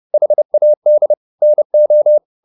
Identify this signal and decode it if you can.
Morse code, heard as a keyed tone, about 30 wpm, 600 Hz. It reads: HAD NO
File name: the old man and the sea / the old man and the sea 1 30wpm